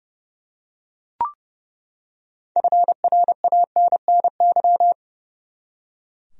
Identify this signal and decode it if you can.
Morse code, heard as a keyed tone, about 30 wpm, 700 Hz. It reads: FRANNY
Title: USA Common Names - 30wpm